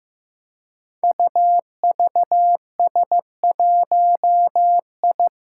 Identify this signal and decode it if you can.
Morse code, heard as a keyed tone, about 15 wpm, 700 Hz. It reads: UVS1I
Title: Five Challenging Letter-Number Combinations 15wpm